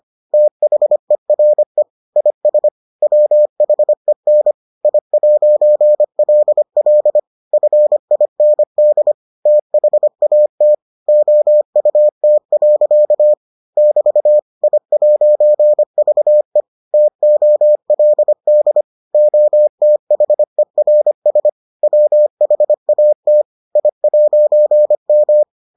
Morse code, heard as a keyed tone, about 25 wpm, 600 Hz. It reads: THERE IS WHEN I'LL FIND THAT OUT. <BT> I'VE TOLD OTHERS WHAT I'M